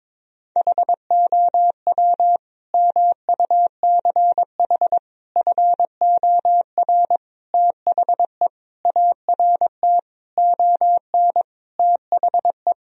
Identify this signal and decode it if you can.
Morse code, heard as a keyed tone, about 22 wpm, 700 Hz. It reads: HOW MUCH FOR THE ART ON THE
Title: Sentences from Top 1000 Words - Review Part 3 - 22wpm